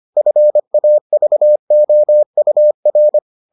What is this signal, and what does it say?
Morse code, heard as a keyed tone, about 25 wpm, 600 Hz. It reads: FAVOUR